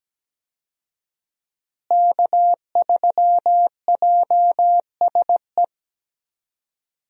Morse code, heard as a keyed tone, about 17 wpm, 700 Hz. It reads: K3JSE
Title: USA Call Signs - 5 Characters - Repeated 3x - 17wpm